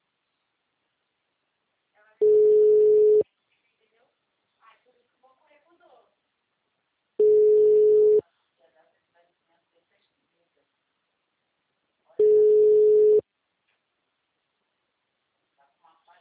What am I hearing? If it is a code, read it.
Ringback tone (Europe, 425 Hz), with speech.
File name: out- -2006- - - .23183